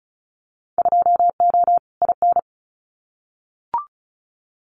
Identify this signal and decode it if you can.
Morse code, heard as a keyed tone, about 34 wpm, 700 Hz. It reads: 2O IN
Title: Winter Field Day Exchanges 35wpm